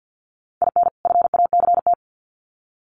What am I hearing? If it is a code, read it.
Morse code, heard as a keyed tone, about 50 wpm, 700 Hz. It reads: IN FACT
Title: Sets of 2 Words 50wpm